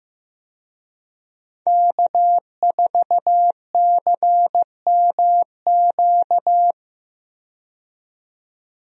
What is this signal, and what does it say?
Morse code, heard as a keyed tone, about 15 wpm, 700 Hz. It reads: K4CMQ
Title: USA Call Signs - 5 Characters - 15wpm